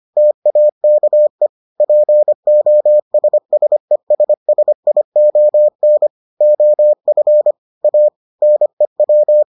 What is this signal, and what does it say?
Morse code, heard as a keyed tone, about 25 wpm, 600 Hz. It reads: TAKE POSSESSION OF A NEW